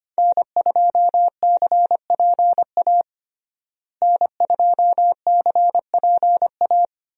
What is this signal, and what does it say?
Morse code, heard as a keyed tone, about 25 wpm, 700 Hz. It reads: N2CPA N2CPA
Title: USA Call Signs - 5 Characters - Repeated 3x - 25wpm